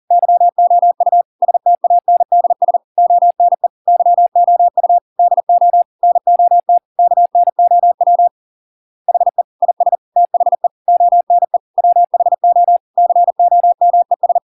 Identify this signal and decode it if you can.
Morse code, heard as a keyed tone, about 40 wpm, 700 Hz. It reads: YOU STANDS ONE YOU DO NOT KNOW HE IS THE ONE WHO COMES